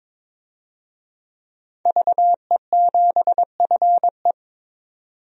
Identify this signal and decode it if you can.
Morse code, heard as a keyed tone, about 22 wpm, 700 Hz. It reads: VE7FE
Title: DXCC-100 Call Signs - 5 Characters - 22wpm